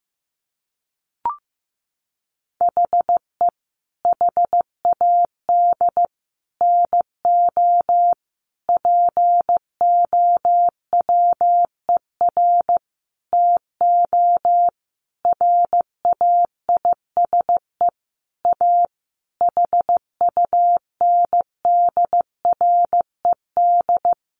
Morse code, heard as a keyed tone, about 15 wpm, 700 Hz. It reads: HE HAD NO POWER TO RAISE A HUNDRED
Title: Sentences from Top 1400 Words 15wpm